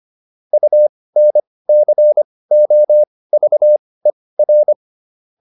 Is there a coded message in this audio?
Yes: Morse code from 0.5 to 4.7 s, heard as a keyed tone, about 25 wpm, spaced as for 18 wpm, 600 Hz. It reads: UNCOVER